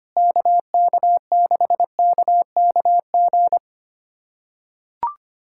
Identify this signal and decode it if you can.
Morse code, heard as a keyed tone, about 25 wpm, 700 Hz. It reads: KK6KKG
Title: USA Callsign DE Callsign - 6 Characters - 25wpm